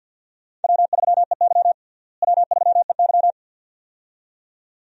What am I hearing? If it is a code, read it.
Morse code, heard as a keyed tone, about 52 wpm, 700 Hz. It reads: W3EY W3EY
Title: USA Call Signs - 4 Characters - Repeated 2x - 50wpm